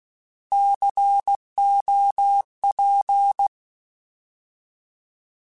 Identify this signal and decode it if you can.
Morse code, heard as a keyed tone, about 16 wpm, 800 Hz. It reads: COP
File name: Alphabetical / Alphabetical mp3s / Cs @15